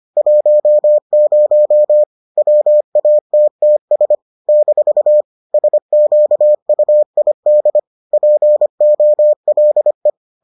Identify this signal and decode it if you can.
Morse code, heard as a keyed tone, about 25 wpm, 600 Hz. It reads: 10 WATTS - SQUID POLE